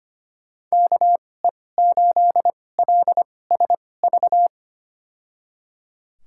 Morse code, heard as a keyed tone, about 25 wpm, spaced as for 18 wpm, 700 Hz. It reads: KE8LSV
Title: USA Call Signs - 2x Character Spacing - 25wpm